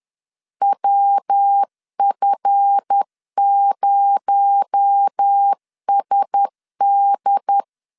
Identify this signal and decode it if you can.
Morse code, heard as a keyed tone, about 10 wpm, 800 Hz. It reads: WF0SD
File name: LEZIONI / 8 YJ45 / 8 YJ4 10wpm 2633